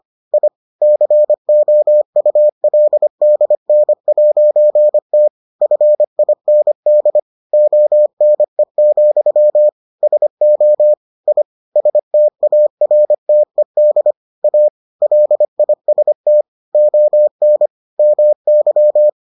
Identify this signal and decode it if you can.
Morse code, heard as a keyed tone, about 25 wpm, 600 Hz. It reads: I COULDN'T FIND ONE, SO I STARTED A LIST ON MY